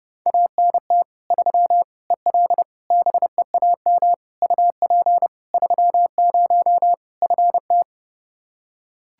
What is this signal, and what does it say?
Morse code, heard as a keyed tone, about 30 wpm, 700 Hz. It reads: ANT 3 EL BEAM UP 30 FT
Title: Practice QSO - Exchange 2 - Info - 30wpm